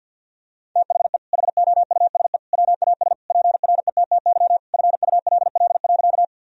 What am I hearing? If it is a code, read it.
Morse code, heard as a keyed tone, about 50 wpm, 700 Hz. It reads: THE HOUSE WAS PRETTY FULL.
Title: Sentences from Top 1000 Words - Review Part 3 - 50wpm